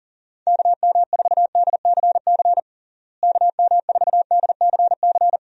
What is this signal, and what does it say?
Morse code, heard as a keyed tone, about 40 wpm, 700 Hz. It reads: KM4DCC KM4DCC
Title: USA Call Signs - 6 Characters - Repeated 3x - 40wpm